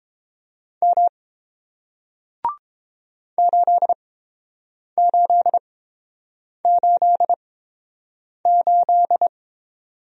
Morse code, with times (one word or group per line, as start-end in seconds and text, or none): 0.8-1.1 M
3.4-9.3 8888